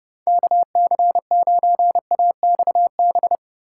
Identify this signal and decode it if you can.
Morse code, heard as a keyed tone, about 40 wpm, 700 Hz. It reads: KC9AXB